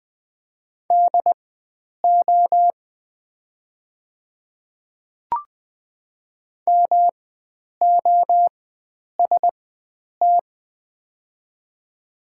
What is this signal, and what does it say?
Morse code, heard as a keyed tone, about 20 wpm, spaced as for 9 wpm, 700 Hz. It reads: DO MOST
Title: 4x Character Spacing - Top 100 Words - 20wpm